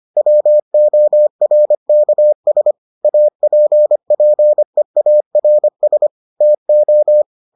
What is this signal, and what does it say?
Morse code, heard as a keyed tone, about 25 wpm, 600 Hz. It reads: WORKS APPEARS TO